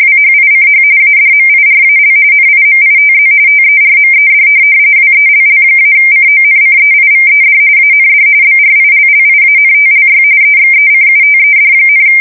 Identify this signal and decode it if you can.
RTTY, about 45.45 baud. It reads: WHICH ADMIRAL NIMITZ TERMED WELL-CONCEIVED AND BRILLIANTLY EXECUTED,
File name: fsk170 / USS-New-Jersey